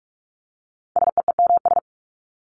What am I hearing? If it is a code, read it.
Morse code, heard as a keyed tone, about 44 wpm, 700 Hz. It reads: SEEMS
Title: ES:S Suffix - Top 100 Words with Base Words 45wpm